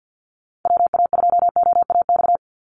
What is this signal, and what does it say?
Morse code, heard as a keyed tone, about 49 wpm, 700 Hz. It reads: WA2OAX
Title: Call Signs Repeated 3x - 50wpm